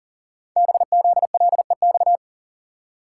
Morse code, heard as a keyed tone, about 40 wpm, 700 Hz. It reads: BZLEX
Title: Code Groups - 5 Letters - 40wpm